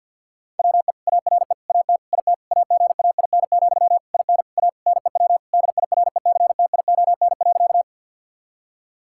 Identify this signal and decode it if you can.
Morse code, heard as a keyed tone, about 50 wpm, 700 Hz. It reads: WE ARE AT IT AGAIN, IN A NEW DIRECTION.
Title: Sentences from Top 1000 Words - Review Part 5 - 50wpm